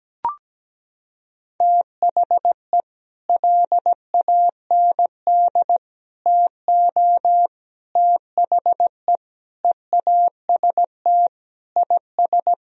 Morse code, heard as a keyed tone, about 17 wpm, 700 Hz. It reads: THE LAND TO THE EAST IS
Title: Sentences from Top 1000 Words - Review Part 4 - 17wpm